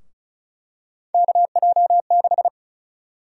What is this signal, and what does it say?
Morse code, heard as a keyed tone, about 34 wpm, 700 Hz. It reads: KJ6
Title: USA Call Sign Prefixes - 35wpm